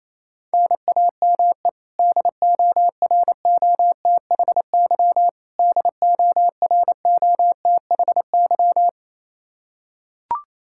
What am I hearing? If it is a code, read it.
Morse code, heard as a keyed tone, about 28 wpm, 700 Hz. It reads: NAME DOROTHY DOROTHY
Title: Practice QSO - Exchange 1 - Beginning - 28wpm